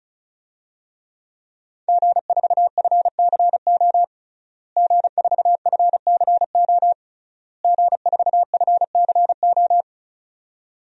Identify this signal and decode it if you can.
Morse code, heard as a keyed tone, about 34 wpm, 700 Hz. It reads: G4FCO G4FCO G4FCO
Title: DXCC-100 Call Signs - 5 Characters - Repeated 3x - 35wpm